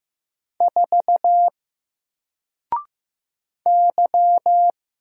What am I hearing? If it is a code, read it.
Morse code, heard as a keyed tone, about 15 wpm, 700 Hz. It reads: 4 Y
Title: Single Letter-Number Alt 15wpm